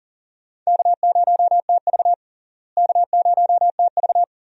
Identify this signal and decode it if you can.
Morse code, heard as a keyed tone, about 40 wpm, 700 Hz. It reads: K0TV K0TV
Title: USA Call Signs - 4 Characters - Repeated 3x - 40wpm